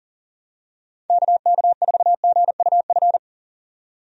Morse code, heard as a keyed tone, about 40 wpm, 700 Hz. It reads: KK4GUF
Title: USA Call Signs - 6 Characters - 40wpm